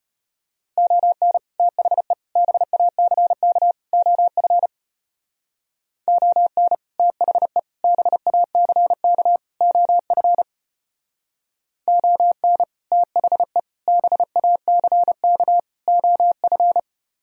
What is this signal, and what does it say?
Morse code, heard as a keyed tone, about 31 wpm, 700 Hz. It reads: ON THE BACK OF ON THE BACK OF ON THE BACK OF
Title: Speed-Racing - Sets of 4 Words 25wpm